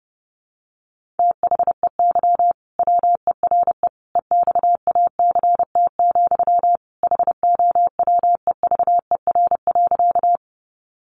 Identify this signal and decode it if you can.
Morse code, heard as a keyed tone, about 30 wpm, 700 Hz. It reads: THEY WERE EXACT, HOWEVER.
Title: Sentences from Top 1300 Words 30wpm